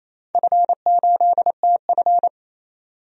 Morse code, heard as a keyed tone, about 28 wpm, 700 Hz. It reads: F8TF
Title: DXCC-100 Call Signs - 4 Characters - 28wpm